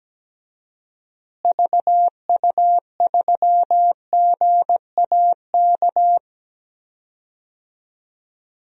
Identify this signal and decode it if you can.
Morse code, heard as a keyed tone, about 17 wpm, 700 Hz. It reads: VU3GAK
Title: DXCC-100 Call Signs - 6 Characters - Repeated 2x - 17wpm